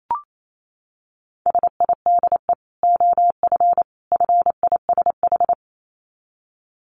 Morse code, heard as a keyed tone, about 28 wpm, 700 Hz. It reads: SIDE OF FISH
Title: Dit Challenge 28wpm